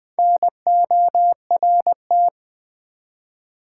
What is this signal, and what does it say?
Morse code, heard as a keyed tone, about 20 wpm, 700 Hz. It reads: NORT
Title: Build A Word - 1 Letter At A Time - 5-Letter Words - 20wpm